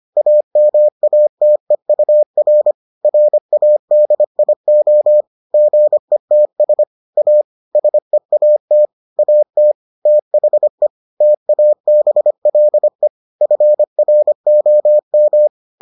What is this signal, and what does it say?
Morse code, heard as a keyed tone, about 25 wpm, 600 Hz. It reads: AMATEUR RADIO GETS A SEAT AT THE TABLE FROM